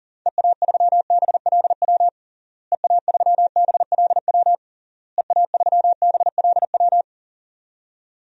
Morse code, heard as a keyed tone, about 40 wpm, 700 Hz. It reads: EA3BLW EA3BLW EA3BLW
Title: DXCC-100 Call Signs - 6 Characters - Repeated 3x - 40wpm